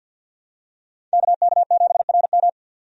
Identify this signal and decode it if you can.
Morse code, heard as a keyed tone, about 52 wpm, 700 Hz. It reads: KK7RM
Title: USA Call Signs - 5 Characters - 50wpm